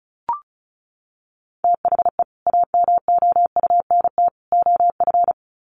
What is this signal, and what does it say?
Morse code, heard as a keyed tone, about 34 wpm, 700 Hz. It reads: THE AMOUNT OF
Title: Sets of 3 Words 35wpm